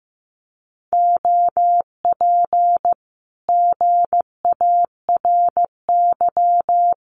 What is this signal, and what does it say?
Morse code, heard as a keyed tone, about 14 wpm, 700 Hz. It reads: OP GARY
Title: Common QSO Phrases Encore 15wpm